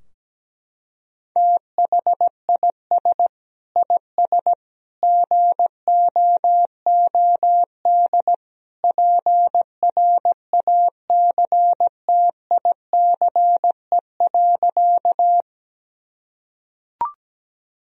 Morse code, heard as a keyed tone, about 17 wpm, 700 Hz. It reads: THIS IS GOOD PRACTICE.
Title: Sentences from Top 1000 Words - Review Part 4 - 17wpm